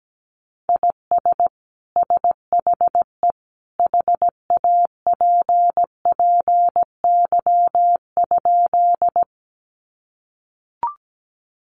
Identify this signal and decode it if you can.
Morse code, heard as a keyed tone, about 17 wpm, 700 Hz. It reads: IS SHE HAPPY?
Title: Sentences from Top 1000 Words 17wpm